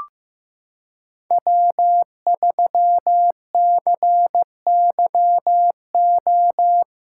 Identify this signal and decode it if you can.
Morse code, heard as a keyed tone, about 15 wpm, 700 Hz. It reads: W3CYO
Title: USA Callsign DE Callsign - 5 Characters - 15wpm